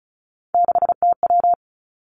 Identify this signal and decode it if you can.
Morse code, heard as a keyed tone, about 36 wpm, 700 Hz. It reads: 6TW
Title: Three Challenging Letter-Number Combinations 35wpm